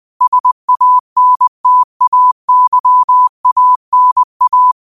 Morse code, heard as a keyed tone, about 20 wpm, 1000 Hz. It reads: SANTAYANA
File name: qotdfiles / 20wpm